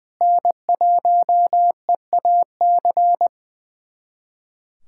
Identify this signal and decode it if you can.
Morse code, heard as a keyed tone, about 20 wpm, 700 Hz. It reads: N1EAC
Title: USA Callsign DE Callsign - 5 Characters - 20wpm